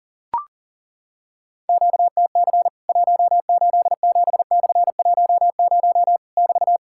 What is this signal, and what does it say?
Morse code, heard as a keyed tone, about 40 wpm, 700 Hz. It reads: QTC 187/10 <BT>
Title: Worked All Europe DX Contest - QTC Exchange - 40wpm